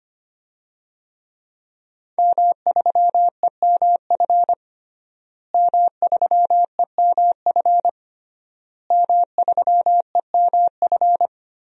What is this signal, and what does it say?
Morse code, heard as a keyed tone, about 25 wpm, 700 Hz. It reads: M3EMF M3EMF M3EMF
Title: DXCC-100 Call Signs - 5 Characters - Repeated 3x - 25wpm